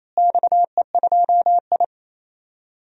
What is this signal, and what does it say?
Morse code, heard as a keyed tone, about 28 wpm, 700 Hz. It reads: XE2I
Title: DXCC-100 Call Signs - 4 Characters - Repeated 3x - 28wpm